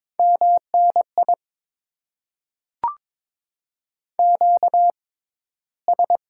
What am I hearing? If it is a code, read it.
Morse code, heard as a keyed tone, about 23 wpm, spaced as for 14 wpm, 700 Hz. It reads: MNI QS
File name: Top 90 QSO Elements - 6x Character Spacing - 22wpm